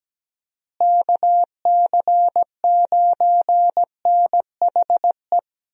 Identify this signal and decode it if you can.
Morse code, heard as a keyed tone, about 17 wpm, 700 Hz. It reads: KC9NHE
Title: USA Call Signs - 6 Characters - Repeated 3x - 17wpm